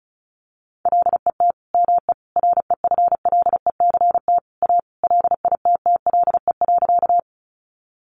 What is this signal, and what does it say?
Morse code, heard as a keyed tone, about 34 wpm, 700 Hz. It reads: LET ME REFLECT A LITTLE.
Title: Sentences from Top 1500 Words 35wpm